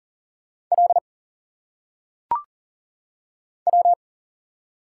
Morse code, heard as a keyed tone, about 40 wpm, 700 Hz. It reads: L W
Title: Single Letters - Phonetic Pronunciation 40wpm